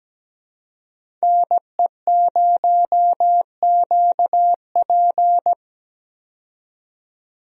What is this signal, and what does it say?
Morse code, heard as a keyed tone, about 17 wpm, 700 Hz. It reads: NE0QP
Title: USA Call Signs - 5 Characters - 17wpm